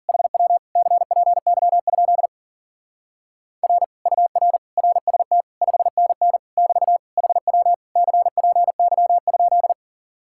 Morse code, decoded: HW CPY? R UR RST 5NN <BT> HW CPY?

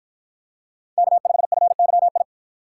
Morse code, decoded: K5FYI

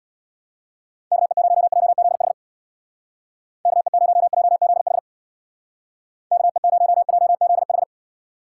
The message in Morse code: KE0JZH KE0JZH KE0JZH

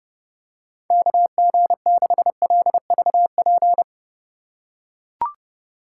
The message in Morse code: KG6LVP

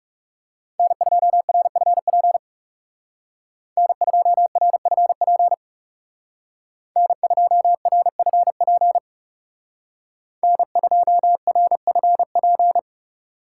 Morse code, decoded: N2RFP N2RFP N2RFP N2RFP

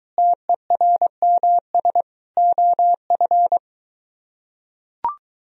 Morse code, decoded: TERMS OF

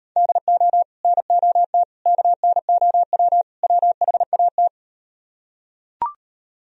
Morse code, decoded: DO NOT KNOW WHAT

